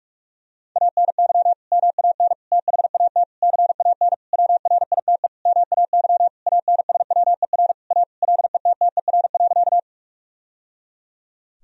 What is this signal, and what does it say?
Morse code, heard as a keyed tone, about 47 wpm, 700 Hz. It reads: ANY MAN THAT CAN WRITE MAY ANSWER A LETTER.
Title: Sentences from Top 1000 Words - Review Part 1 - 45wpm